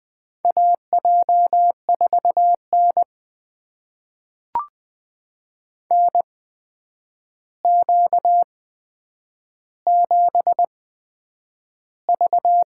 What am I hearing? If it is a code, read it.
Morse code, heard as a keyed tone, about 20 wpm, spaced as for 9 wpm, 700 Hz. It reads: AJ4N NQ7V
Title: USA Call Signs - 4 Characters - 8x Character Spacing - 20wpm